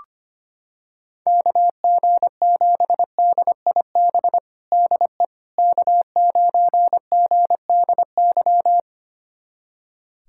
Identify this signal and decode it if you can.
Morse code, heard as a keyed tone, about 25 wpm, 700 Hz. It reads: KG7DIB DE K9GDY
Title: USA Callsign DE Callsign 25wpm